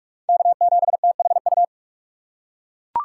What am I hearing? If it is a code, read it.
Morse code, heard as a keyed tone, about 43 wpm, 700 Hz. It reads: K7THU
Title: USA Call Signs - 5 Characters - 2x Character Spacing - 45wpm